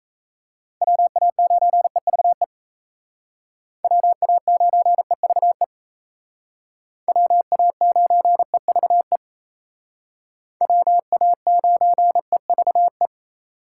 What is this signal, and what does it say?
Morse code, heard as a keyed tone, about 36 wpm, 700 Hz. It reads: WA9EVE WA9EVE WA9EVE WA9EVE